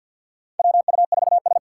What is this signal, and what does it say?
Morse code, heard as a keyed tone, about 53 wpm, 700 Hz. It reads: WU4S
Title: USA Call Signs - 2x Character Spacing - 50wpm